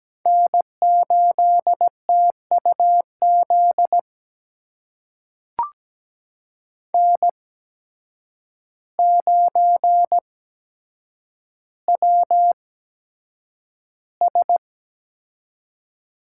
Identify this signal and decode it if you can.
Morse code, heard as a keyed tone, about 17 wpm, spaced as for 8 wpm, 700 Hz. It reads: N8TUZ N9WS
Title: USA Call Signs - 5 Characters - 8x Character Spacing - 17wpm